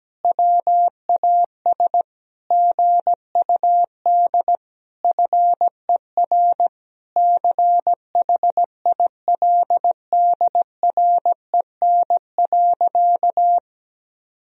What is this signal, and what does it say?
Morse code, heard as a keyed tone, about 17 wpm, 700 Hz. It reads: WAS GUD FER CHILDREN.
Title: Sentences from Top 400 Words and Abbreviations 17wpm